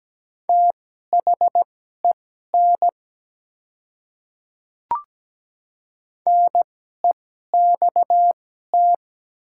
Morse code, heard as a keyed tone, about 17 wpm, spaced as for 12 wpm, 700 Hz. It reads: THEN NEXT
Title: 2x Character Spacing - Top 300 Words - 17wpm